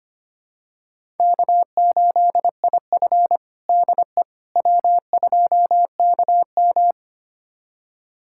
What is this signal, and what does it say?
Morse code, heard as a keyed tone, about 25 wpm, 700 Hz. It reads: K8IF DE W2KM
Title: USA Callsign DE Callsign - 4 Characters - 25wpm